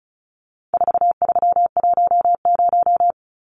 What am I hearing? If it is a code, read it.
Morse code, heard as a keyed tone, about 34 wpm, 700 Hz. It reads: 4310
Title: Numbers - 4-Digit 35wpm